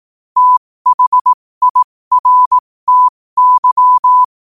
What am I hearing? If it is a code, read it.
Morse code, heard as a keyed tone, about 18 wpm, spaced as for 15 wpm, 1000 Hz. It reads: THIRTY
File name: qotdfiles / 15wpm